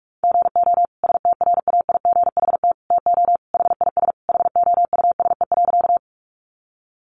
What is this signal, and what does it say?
Morse code, heard as a keyed tone, about 44 wpm, 700 Hz. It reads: GO STRAIGHT TO HIS HOUSE.
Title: Sentences from Top 1000 Words 45wpm